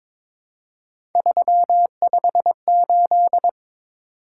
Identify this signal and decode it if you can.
Morse code, heard as a keyed tone, about 23 wpm, 700 Hz. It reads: 358
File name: practice / 22wpm / Three Letter-Number Combinations - 22wpm